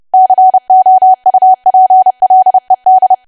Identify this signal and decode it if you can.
Morse code, heard as a keyed tone, about 30 wpm, 750 Hz. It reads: COUPLED
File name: cwa / W1AW-Level3 / 30WPM